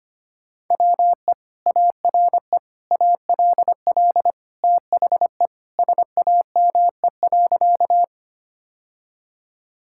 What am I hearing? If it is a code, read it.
Morse code, heard as a keyed tone, about 25 wpm, 700 Hz. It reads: WE ARE ALL THE SAME.
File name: practice / 25wpm / Sentences with 2-Word Phrases - 25wpm